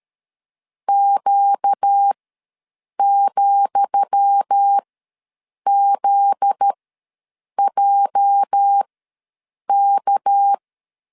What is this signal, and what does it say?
Morse code, heard as a keyed tone, about 13 wpm, spaced as for 7 wpm, 800 Hz. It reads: Q,ZJK
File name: LEZIONI / 10 SIGNS / 10D COMA ws 12wpm 3618